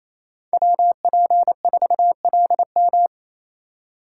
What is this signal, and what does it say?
Morse code, heard as a keyed tone, about 28 wpm, 700 Hz. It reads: WP4LM
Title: USA Call Signs - 4x Character Spacing - 28wpm